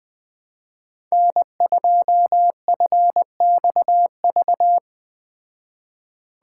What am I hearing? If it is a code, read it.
Morse code, heard as a keyed tone, about 20 wpm, 700 Hz. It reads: N2FXV